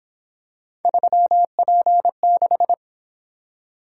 Morse code, heard as a keyed tone, about 26 wpm, 700 Hz. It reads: 3P6
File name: Speed-Racing - Three Letter-Number Combinations - 22wpm